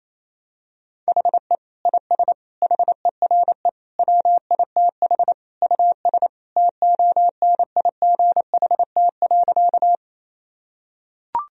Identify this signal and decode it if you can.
Morse code, heard as a keyed tone, about 28 wpm, 700 Hz. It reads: HE IS HERE WITH US TONIGHT.
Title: Sentences from Top 1000 Words - Review Part 3 - 28wpm